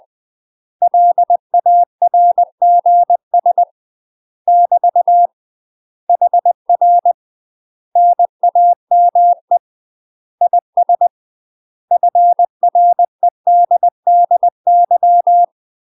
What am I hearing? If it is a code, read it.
Morse code, heard as a keyed tone, about 20 wpm, 700 Hz. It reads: LARGS <BT> HR NAME IS FREDDY
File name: UK MASTER QSOs TRAINING/20WPM